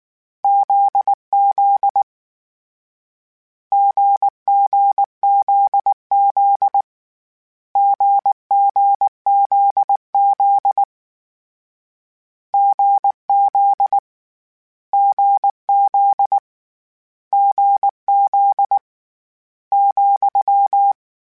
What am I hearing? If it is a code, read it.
Morse code, heard as a keyed tone, about 19 wpm, 800 Hz. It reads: ZZ GGZZ GGZZ GZ GZ GZ ,
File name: Difficult Combos / Difficult Combos-mp3s / GZ GZ7 @18